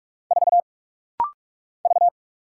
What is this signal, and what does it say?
Morse code, heard as a keyed tone, about 46 wpm, 700 Hz. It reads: 4 V
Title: Single Letter-Number - Rapid-Fire 45wpm